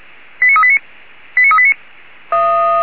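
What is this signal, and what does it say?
Five-tone selective call (ZVEI), heard as quick runs of five tones, tones of about 70 ms. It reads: 89389 89389
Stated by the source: ZVEI five-tone selective call received on SDR.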